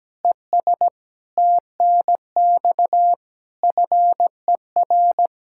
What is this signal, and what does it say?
Morse code, heard as a keyed tone, about 17 wpm, 700 Hz. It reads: ES TNX FER
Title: Practice QSO - Exchange 2 - Info - 17wpm